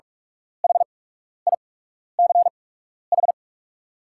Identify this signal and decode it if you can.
Morse code, heard as a keyed tone, about 44 wpm, spaced as for 12 wpm, 690 Hz. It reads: HICH
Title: 8x Character Spacing - Top 100 Words - 45wpm